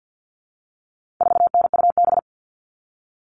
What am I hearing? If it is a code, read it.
Morse code, heard as a keyed tone, about 50 wpm, 700 Hz. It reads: 4NUB
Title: All Cut Numbers - 4-Digit 50wpm